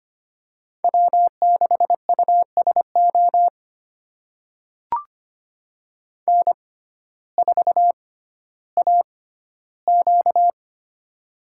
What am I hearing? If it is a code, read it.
Morse code, heard as a keyed tone, about 25 wpm, spaced as for 14 wpm, 700 Hz. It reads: W6USO N4AQ
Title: USA Call Signs - 6x Character Spacing - 25wpm